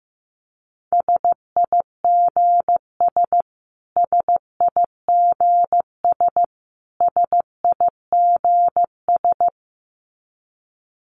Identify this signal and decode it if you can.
Morse code, heard as a keyed tone, about 15 wpm, 700 Hz. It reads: SIGS SIGS SIGS
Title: Abbreviations Repeated 3x - 15wpm